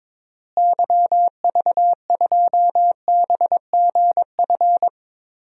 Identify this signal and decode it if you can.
Morse code, heard as a keyed tone, about 22 wpm, 700 Hz. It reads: YV2BGF
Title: DXCC-100 Call Signs - 6 Characters - Repeated 2x - 22wpm